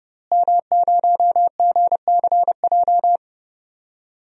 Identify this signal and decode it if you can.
Morse code, heard as a keyed tone, about 32 wpm, 700 Hz. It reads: M0GCJ
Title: DXCC-100 Call Signs - 5 Characters - Repeated 2x - 30wpm